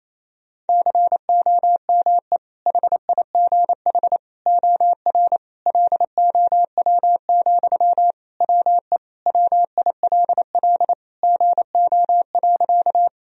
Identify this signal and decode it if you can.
Morse code, heard as a keyed tone, about 28 wpm, 700 Hz. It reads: COME HIGH OR LOW, WE WILL GO.
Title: Sentences from Top 1000 Words - Review Part 1 - 28wpm